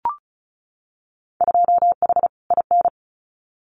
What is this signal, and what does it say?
Morse code, heard as a keyed tone, about 34 wpm, 700 Hz. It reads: 2H IN
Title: Winter Field Day Exchanges Encore 35wpm